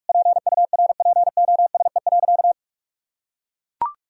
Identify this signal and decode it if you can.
Morse code, heard as a keyed tone, about 41 wpm, 700 Hz. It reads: PURPOSE.
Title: Sentences from Top 1000 Words - Review Part 2 - 45wpm